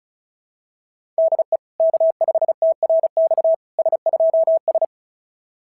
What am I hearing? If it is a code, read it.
Morse code, heard as a keyed tone, about 36 wpm, 690 Hz. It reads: DE K5TRX S2S
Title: SOTA QSO Practice 35wpm